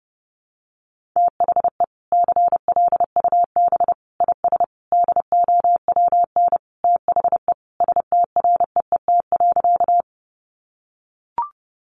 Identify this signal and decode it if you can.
Morse code, heard as a keyed tone, about 30 wpm, 700 Hz. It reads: THE CLUB IS DOWN THE STREET.
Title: Sentences from Top 1200 Words 30wpm